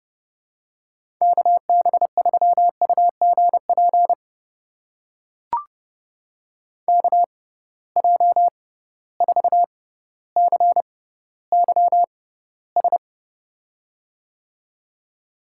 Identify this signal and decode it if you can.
Morse code, heard as a keyed tone, about 30 wpm, spaced as for 15 wpm, 700 Hz. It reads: KB3UGP KJ4CYS
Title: USA Call Signs - 6 Characters - 6x Character Spacing - 30wpm